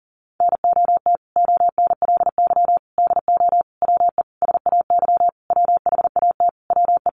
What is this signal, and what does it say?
Morse code, heard as a keyed tone, about 40 wpm, 700 Hz. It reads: NOT ONLY DO WE SAY WHAT WE
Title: Sentences from Top 1200 Words 40wpm